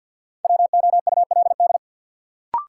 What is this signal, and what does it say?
Morse code, heard as a keyed tone, about 48 wpm, 700 Hz. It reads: WOULD
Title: Sentences from Top 100 Words - New Word Drill - 50wpm